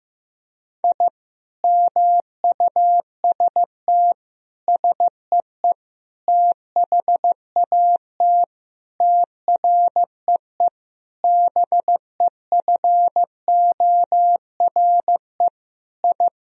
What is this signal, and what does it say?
Morse code, heard as a keyed tone, about 15 wpm, 700 Hz. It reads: I MUST SEE THAT TREE BEFORE I
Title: Sentences from Top 1000 Words - Review Part 1 - 15wpm